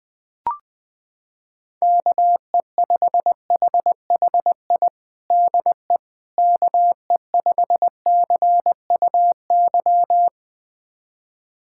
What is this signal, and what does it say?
Morse code, heard as a keyed tone, about 20 wpm, 700 Hz. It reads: KE5HHI DE KE5CUY
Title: USA Callsign DE Callsign - 6 Characters - 20wpm